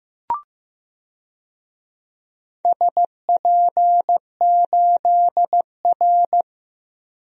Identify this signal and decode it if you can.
Morse code, heard as a keyed tone, about 15 wpm, 700 Hz. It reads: SP8R
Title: DXCC-100 Call Signs - 4 Characters - Repeated 3x - 15wpm